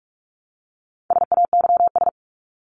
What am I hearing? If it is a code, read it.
Morse code, heard as a keyed tone, about 44 wpm, 700 Hz. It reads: SAYS
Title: ES:S Suffix - Top 100 Words with Base Words 45wpm